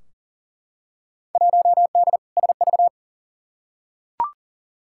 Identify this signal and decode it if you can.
Morse code, heard as a keyed tone, about 40 wpm, 700 Hz. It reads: 1D SV
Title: Field Day Exchanges - 40wpm